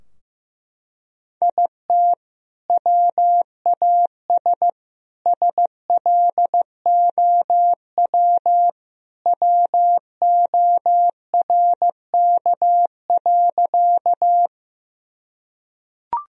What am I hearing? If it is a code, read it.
Morse code, heard as a keyed tone, about 16 wpm, 700 Hz. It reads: IT WAS SLOW WORK.
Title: Sentences from Top 1000 Words - Review Part 2 - 15wpm